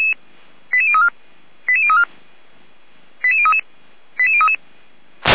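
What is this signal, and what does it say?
Five-tone selective call (ZVEI), heard as quick runs of five tones, tones of about 70 ms. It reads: 88034 88034 88033 88033
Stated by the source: ZVEI five-tone selective call 89117 received via SDR.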